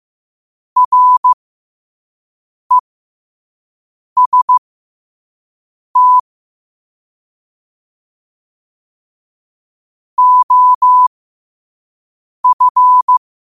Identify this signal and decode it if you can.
Morse code, heard as a keyed tone, about 15 wpm, spaced as for 5 wpm, 1000 Hz. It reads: REST OF
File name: qotdfiles / 5wpm